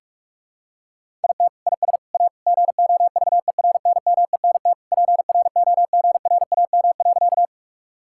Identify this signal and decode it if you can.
Morse code, heard as a keyed tone, about 46 wpm, 700 Hz. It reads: IT IS A GOVERNMENT PROGRAM.
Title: Sentences from Top 1000 Words - Review Part 5 - 45wpm